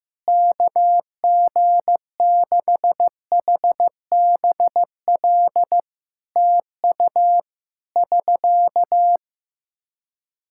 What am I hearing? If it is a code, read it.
Morse code, heard as a keyed tone, about 15 wpm, 700 Hz. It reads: KG6HBL TU <SK>
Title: Practice QSO - Exchange 4 - Ending - 15wpm